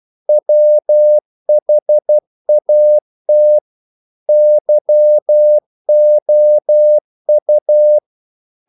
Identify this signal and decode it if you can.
Morse code, heard as a keyed tone, about 12 wpm, 600 Hz. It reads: WHAT YOU
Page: English quotations in CW (telegraphy - morse)